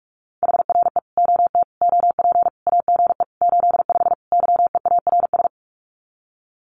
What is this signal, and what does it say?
Morse code, heard as a keyed tone, about 44 wpm, 700 Hz. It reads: HRE OT OP AGE 85 YEARS
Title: Common QSO Phrases 45wpm